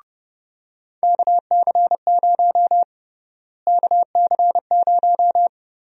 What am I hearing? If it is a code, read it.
Morse code, heard as a keyed tone, about 31 wpm, 700 Hz. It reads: KC0 KC0
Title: USA Call Sign Prefixes - Repeated 2x - 30wpm